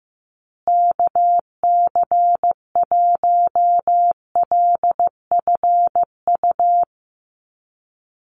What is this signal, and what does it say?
Morse code, heard as a keyed tone, about 15 wpm, 700 Hz. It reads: KC1LFU
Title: USA Call Signs Repeated 2x 15wpm